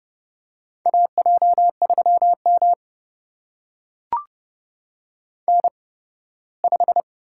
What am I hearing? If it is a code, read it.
Morse code, heard as a keyed tone, about 30 wpm, spaced as for 18 wpm, 700 Hz. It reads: AJ3M N5
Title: USA Call Signs - 8x Character Spacing - 30wpm